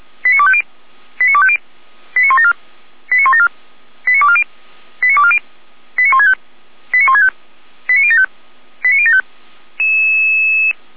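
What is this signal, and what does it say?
Five-tone selective call (ZVEI), heard as quick runs of five tones, tones of about 70 ms. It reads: 89250 89250 89174 89174 89240 89240 89165 89165 89075 89075
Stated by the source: ZVEI five-tone selective call 89173 received by SDR, logged on the Feuerwehr Orlamünde (fire brigade) audio server.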